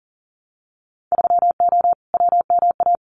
Morse code, heard as a keyed tone, about 40 wpm, 700 Hz. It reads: 3O WMA
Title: Winter Field Day Exchanges Encore 40wpm